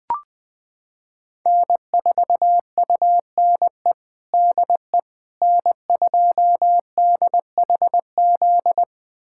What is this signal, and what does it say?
Morse code, heard as a keyed tone, about 20 wpm, 700 Hz. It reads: N4UNE DE N2DHZ
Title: USA Callsign DE Callsign - 5 Characters - 20wpm